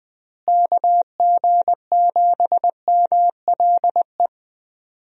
Morse code, heard as a keyed tone, about 20 wpm, 700 Hz. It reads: KG7MLE
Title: USA Call Signs - 6 Characters - Repeated 2x - 20wpm